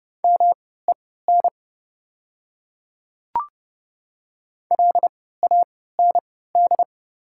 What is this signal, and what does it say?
Morse code, heard as a keyed tone, about 30 wpm, spaced as for 17 wpm, 700 Hz. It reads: MEN LAND
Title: 3x Character Spacing - Top 300 Words - 30wpm